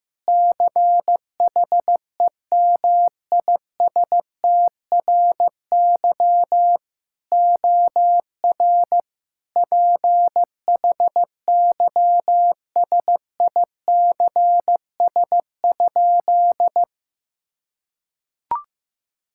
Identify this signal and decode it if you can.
Morse code, heard as a keyed tone, about 15 wpm, 700 Hz. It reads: CHEMISTRY OR PHYSICS?
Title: Sentences from Top 1000 Words - Review Part 6 - 15wpm